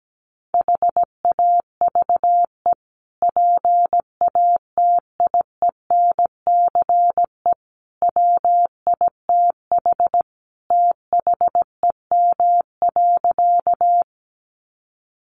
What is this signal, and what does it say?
Morse code, heard as a keyed tone, about 17 wpm, 700 Hz. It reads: HAVE PATIENCE WITH THEM.
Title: Sentences from Top 1300 Words 17wpm